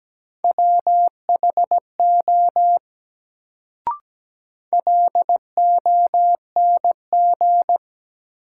Morse code, heard as a keyed tone, about 17 wpm, 700 Hz. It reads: WHO LONG
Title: Sets of 2 Words - New Word Drill - 17wpm